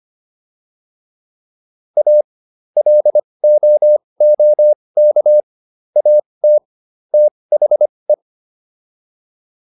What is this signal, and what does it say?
Morse code, heard as a keyed tone, about 25 wpm, spaced as for 20 wpm, 600 Hz. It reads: A LOOK AT THE